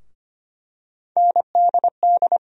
Morse code, heard as a keyed tone, about 25 wpm, 700 Hz. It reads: NDD
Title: USA Call Sign Postfixes - 3 Letters 25wpm